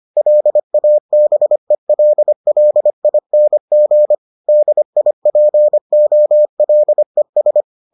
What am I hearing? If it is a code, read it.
Morse code, heard as a keyed tone, about 25 wpm, 600 Hz. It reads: LABELLING DIPOLES